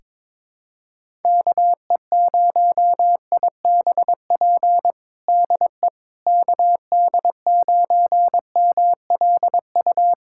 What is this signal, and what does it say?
Morse code, heard as a keyed tone, about 22 wpm, 700 Hz. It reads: KE0IBP DE KD9MLU
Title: USA Callsign DE Callsign - 6 Characters - 22wpm